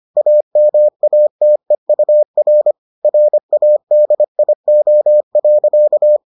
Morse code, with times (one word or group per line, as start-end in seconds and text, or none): 0.2-2.7 AMATEUR
3.0-6.2 RADIO.